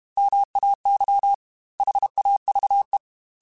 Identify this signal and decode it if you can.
Morse code, heard as a keyed tone, about 32 wpm, 800 Hz. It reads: MAY HAVE
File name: Chuckles / Chuckles-mp3s / SoftWords @30